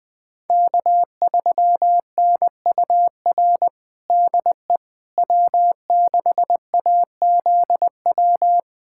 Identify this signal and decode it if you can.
Morse code, heard as a keyed tone, about 21 wpm, 700 Hz. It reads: K3NUR DE W6AZW
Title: USA Callsign DE Callsign - 5 Characters - 20wpm